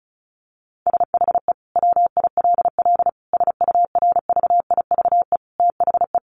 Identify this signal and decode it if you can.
Morse code, heard as a keyed tone, about 34 wpm, 700 Hz. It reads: SHE WILL SURVIVE THE
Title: Sentences from Top 1300 Words 35wpm